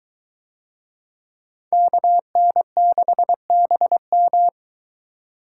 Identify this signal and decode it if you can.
Morse code, heard as a keyed tone, about 21 wpm, 700 Hz. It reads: KN6BM